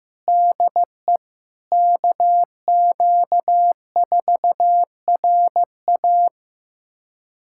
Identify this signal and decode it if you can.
Morse code, heard as a keyed tone, about 15 wpm, 700 Hz. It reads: DE KQ4RA
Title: USA Callsign DE Callsign - 5 Characters - 15wpm